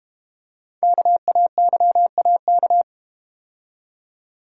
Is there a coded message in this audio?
Yes: Morse code from 0.8 to 2.8 s, heard as a keyed tone, about 32 wpm, 700 Hz. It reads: KAYAK